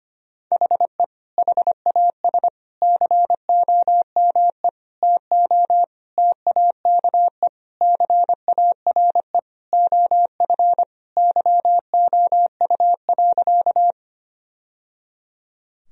Morse code, heard as a keyed tone, about 25 wpm, 700 Hz. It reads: HE HAS COME TO TAKE CARE OF YOU.
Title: Sentences from Top 1000 Words - Review Part 1 - 25wpm